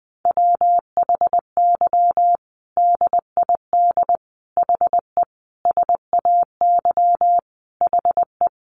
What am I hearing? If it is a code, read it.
Morse code, heard as a keyed tone, about 20 wpm, 700 Hz. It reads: WHY DID HE SAY HE
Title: Sentences from Top 1500 Words 20wpm